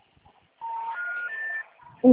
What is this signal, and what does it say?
Special information tone, with noise.